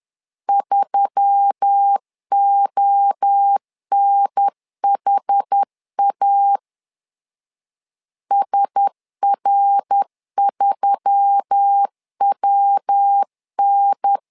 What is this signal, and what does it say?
Morse code, heard as a keyed tone, about 10 wpm, 800 Hz. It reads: 3ONHA SR3WN